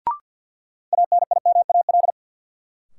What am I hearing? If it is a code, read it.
Morse code, heard as a keyed tone, about 53 wpm, 700 Hz. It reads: ANIMAL
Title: Sentences from Top 1000 Words - Review Part 1 - New Word Drill - 50wpm